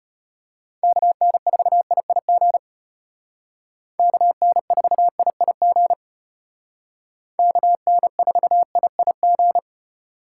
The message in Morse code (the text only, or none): KN4IIG KN4IIG KN4IIG